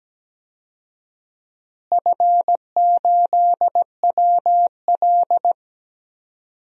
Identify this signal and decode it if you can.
Morse code, heard as a keyed tone, about 17 wpm, 700 Hz. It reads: F8WL
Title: DXCC-100 Call Signs - 4 Characters - Repeated 2x - 17wpm